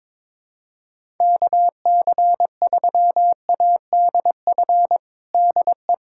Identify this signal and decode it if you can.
Morse code, heard as a keyed tone, about 22 wpm, 700 Hz. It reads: KC3ADF DE
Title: USA Callsign DE Callsign 22wpm